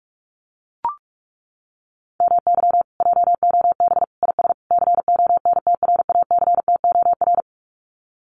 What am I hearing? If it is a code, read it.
Morse code, heard as a keyed tone, about 44 wpm, 700 Hz. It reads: MY JOB IS CONTRACTOR
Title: Common QSO Phrases Encore 45wpm